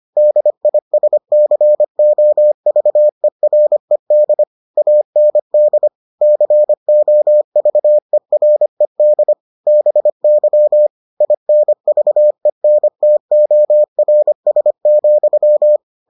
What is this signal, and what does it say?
Morse code, heard as a keyed tone, about 25 wpm, 600 Hz. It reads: DISCOVERED AND COVERED BY INVENTORS,